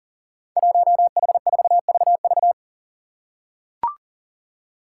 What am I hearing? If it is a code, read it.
Morse code, heard as a keyed tone, about 40 wpm, 700 Hz. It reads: 1H4VV
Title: Five Challenging Letter-Number Combinations 40wpm